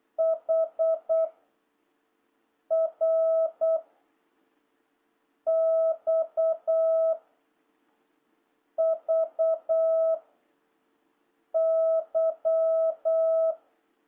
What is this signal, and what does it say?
Morse code, heard as a keyed tone, about 8 wpm, 660 Hz. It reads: H R X V Y
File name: jh8cbh / 2026cw